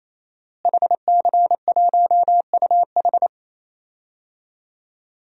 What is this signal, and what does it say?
Morse code, heard as a keyed tone, about 27 wpm, 700 Hz. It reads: HC1UH
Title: DXCC-100 Call Signs - 5 Characters - Repeated 3x - 28wpm